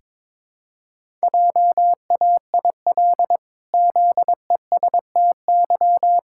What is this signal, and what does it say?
Morse code, heard as a keyed tone, about 22 wpm, 700 Hz. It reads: JAIL ZESTY
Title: Pangram Challenge v2 - 22wpm